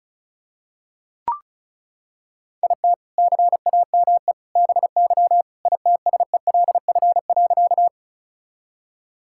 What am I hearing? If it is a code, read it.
Morse code, heard as a keyed tone, about 36 wpm, 700 Hz. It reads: IT CAME BY ITSELF.
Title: Sentences from Top 1000 Words - Review Part 5 - 35wpm